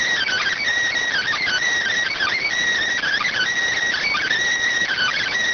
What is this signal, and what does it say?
Emergency Alert System (SAME) tones (header bursts), heard as harsh buzzing bursts. It is heard from 1.1 to 5.3 s.